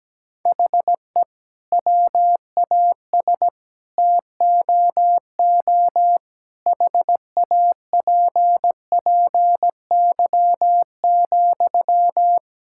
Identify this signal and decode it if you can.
Morse code, heard as a keyed tone, about 17 wpm, 700 Hz. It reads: HE WAS TOO HAPPY,
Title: Sentences from Top 1000 Words - Review Part 6 - 17wpm